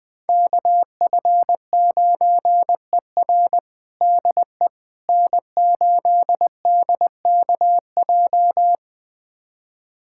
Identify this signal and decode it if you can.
Morse code, heard as a keyed tone, about 21 wpm, 700 Hz. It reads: KF9ER DE N8DKJ
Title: USA Callsign DE Callsign 20wpm